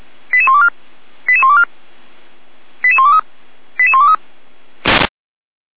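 Five-tone selective call (ZVEI), heard as quick runs of five tones, tones of about 70 ms. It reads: 88124 88124 88123 88123
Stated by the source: Five-tone ZVEI selective call received on an SDR.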